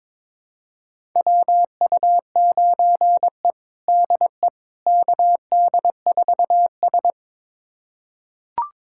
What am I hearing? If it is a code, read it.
Morse code, heard as a keyed tone, about 21 wpm, 700 Hz. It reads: WU9E DE KD4S
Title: USA Callsign DE Callsign - 4 Characters - 22wpm